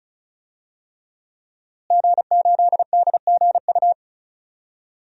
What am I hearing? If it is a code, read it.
Morse code, heard as a keyed tone, about 36 wpm, 700 Hz. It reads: G8DGU
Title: DXCC-100 Call Signs - 5 Characters - Repeated 3x - 35wpm